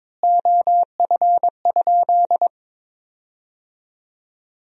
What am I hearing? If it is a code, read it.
Morse code, heard as a keyed tone, about 22 wpm, 700 Hz. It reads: OF?
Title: 4x Word-Spacing - Sentences from Top 200 Words - 22wpm